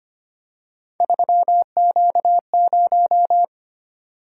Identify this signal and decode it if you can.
Morse code, heard as a keyed tone, about 25 wpm, 700 Hz. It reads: 3Q0
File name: Three Letter-Number Combinations - 25wpm